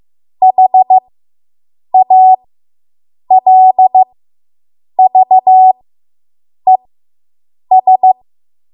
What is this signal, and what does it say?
Morse code, heard as a keyed tone, about 15 wpm, spaced as for 7 wpm, 750 Hz. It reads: HALVES